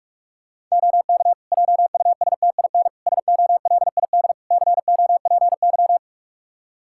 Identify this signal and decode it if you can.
Morse code, heard as a keyed tone, about 46 wpm, 700 Hz. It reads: OK JUSTIN SOLID COPY
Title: Practice QSO - Exchange 3 - More Info - 45wpm